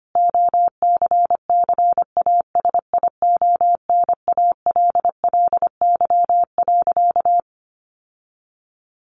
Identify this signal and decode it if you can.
Morse code, heard as a keyed tone, about 25 wpm, 700 Hz. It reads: OCCASIONALLY.
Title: Sentences from Top 1300 Words 25wpm